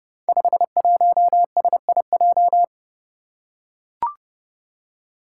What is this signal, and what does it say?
Morse code, heard as a keyed tone, about 30 wpm, 700 Hz. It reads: 51SIJ
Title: Five Challenging Letter-Number Combinations 30wpm